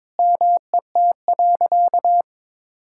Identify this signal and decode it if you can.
Morse code, heard as a keyed tone, about 22 wpm, 700 Hz. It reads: MET.
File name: practice / 22wpm / Sentences with 2-Word Phrases - 22wpm